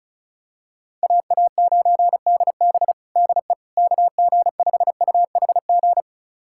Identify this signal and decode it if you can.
Morse code, heard as a keyed tone, about 34 wpm, 700 Hz. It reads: AA9DB DE KG5UHG
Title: USA Callsign DE Callsign 35wpm